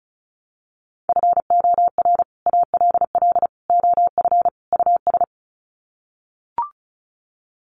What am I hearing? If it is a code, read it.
Morse code, heard as a keyed tone, about 34 wpm, 700 Hz. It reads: FOR ALL OF US
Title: Sets of 4 Words 35wpm